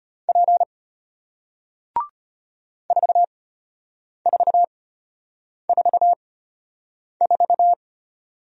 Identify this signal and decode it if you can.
Morse code, heard as a keyed tone, about 34 wpm, spaced as for 8 wpm, 700 Hz. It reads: P 4444